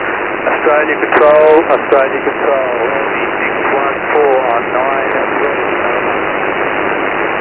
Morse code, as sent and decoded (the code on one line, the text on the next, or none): none